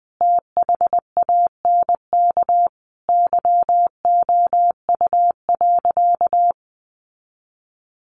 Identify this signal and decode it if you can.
Morse code, heard as a keyed tone, about 20 wpm, 700 Hz. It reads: THANK YOU.
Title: Sentences from Top 600 Words 20wpm